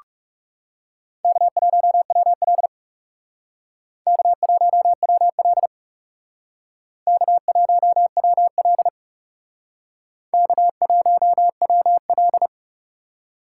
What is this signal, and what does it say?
Morse code, heard as a keyed tone, about 38 wpm, 700 Hz. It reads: K1WL K1WL K1WL K1WL
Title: Speed Racing - USA Call Signs 30wpm